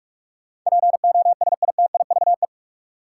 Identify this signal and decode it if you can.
Morse code, heard as a keyed tone, about 44 wpm, 700 Hz. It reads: POSITIVE
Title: Sentences from Top 1000 Words - Review Part 5 - New Word Drill - 45wpm